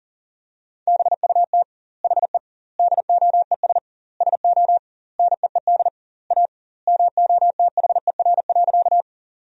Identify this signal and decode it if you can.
Morse code, heard as a keyed tone, about 40 wpm, 700 Hz. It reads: BUT HE DOES SO NEED A MOTHER.
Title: 2x Word-Spacing - Sentences from Top 200 Words - 40wpm